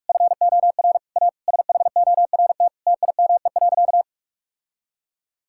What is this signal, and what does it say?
Morse code, heard as a keyed tone, about 44 wpm, 700 Hz. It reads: FOR A SHORT TIME.
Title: Sentences from Top 1600 Words 45wpm